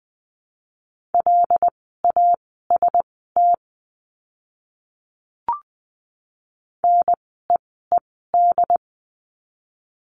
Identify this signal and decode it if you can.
Morse code, heard as a keyed tone, about 20 wpm, spaced as for 14 wpm, 700 Hz. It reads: LAST NEED
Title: 101 Lesson 17f - Letter u - Words 20wpm